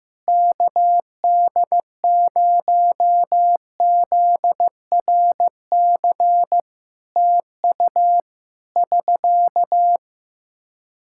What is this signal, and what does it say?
Morse code, heard as a keyed tone, about 15 wpm, 700 Hz. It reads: KD0ZRC TU <SK>
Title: K1USN SST QSO Practice 15wpm